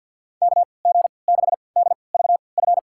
Morse code, heard as a keyed tone, about 52 wpm, spaced as for 29 wpm, 700 Hz. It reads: KG6DVF
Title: USA Call Signs - 6 Characters - 3x Character Spacing - 50wpm